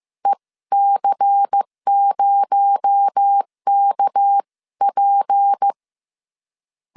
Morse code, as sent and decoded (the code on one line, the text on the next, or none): . -.-. ----- -.- .--.
EC0KP